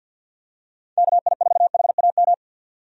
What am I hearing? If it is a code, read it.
Morse code, heard as a keyed tone, about 53 wpm, 700 Hz. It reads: KI4HAM
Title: USA Call Signs - 6 Characters - 50wpm